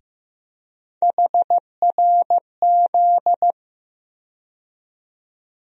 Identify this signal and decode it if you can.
Morse code, heard as a keyed tone, about 15 wpm, 700 Hz. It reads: HRZ
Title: USA Call Sign Postfixes - 3 Letters 15wpm